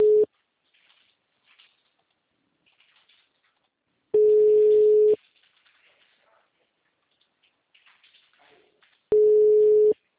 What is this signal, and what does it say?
Ringback tone (Europe, 425 Hz).
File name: out- -2006- - - .30206